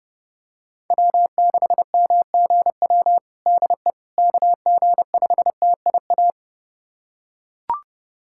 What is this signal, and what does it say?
Morse code, heard as a keyed tone, about 30 wpm, 690 Hz. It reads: W6MGW DE KG5TIA
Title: USA Callsign DE Callsign 30wpm